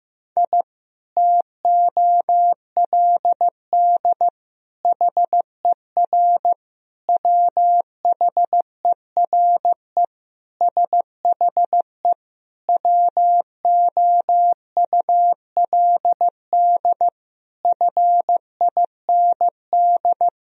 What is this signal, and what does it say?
Morse code, heard as a keyed tone, about 15 wpm, 700 Hz. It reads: I TOLD HER WHERE SHE WOULD FIND
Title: Sentences from Top 1000 Words - Review Part 2 - 15wpm